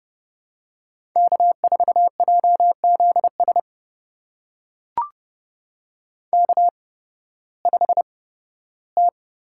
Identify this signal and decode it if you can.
Morse code, heard as a keyed tone, about 32 wpm, spaced as for 16 wpm, 700 Hz. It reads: K4JZS K5T
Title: USA Call Signs - 5 Characters - 8x Character Spacing - 30wpm